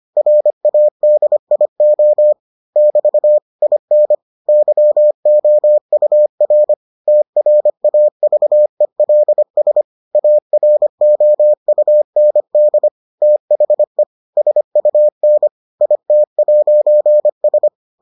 Morse code, heard as a keyed tone, about 25 wpm, 600 Hz. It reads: RADIO <BT> IN YOUR TRAVELS AROUND THE SUN IT'S